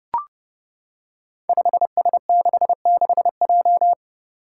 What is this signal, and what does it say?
Morse code, heard as a keyed tone, about 30 wpm, 700 Hz. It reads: 5S66J
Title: Five Challenging Letter-Number Combinations 30wpm